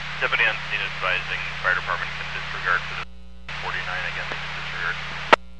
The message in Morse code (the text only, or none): none